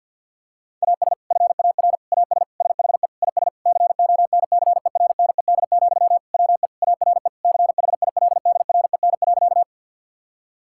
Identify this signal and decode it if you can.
Morse code, heard as a keyed tone, about 50 wpm, 700 Hz. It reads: AS FAR AS SHE IS CONCERNED, WE ARE CHILDREN.
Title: Sentences from Top 1000 Words - Review Part 6 - 50wpm